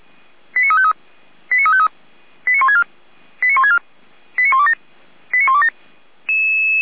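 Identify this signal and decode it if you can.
Five-tone selective call (ZVEI), heard as quick runs of five tones, tones of about 70 ms. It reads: 89353 89353 89164 89164 89127 89127
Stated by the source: ZVEI five-tone selective call received by SDR, on the feuerwehr-orlamuende.org fire brigade server.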